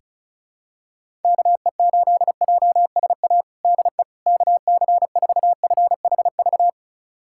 Morse code, heard as a keyed tone, about 36 wpm, 700 Hz. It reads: KE8JSA DE KC4FHV
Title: USA Callsign DE Callsign - 6 Characters - 35wpm